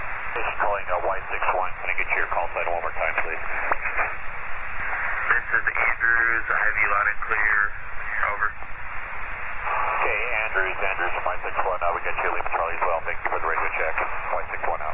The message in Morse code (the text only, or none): none